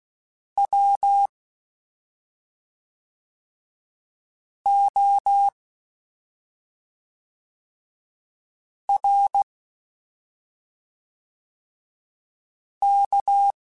Morse code, heard as a keyed tone, about 16 wpm, 800 Hz. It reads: W O R K
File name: Special INGs / Special INGs-mp3s / SlowToFaster ing @5